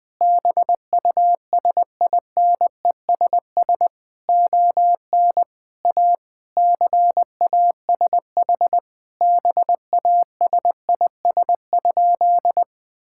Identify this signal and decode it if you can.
Morse code, heard as a keyed tone, about 19 wpm, 700 Hz. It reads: BUSINESS ON A CASH BASIS?
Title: Sentences from Top 1000 Words - Review Part 5 - 20wpm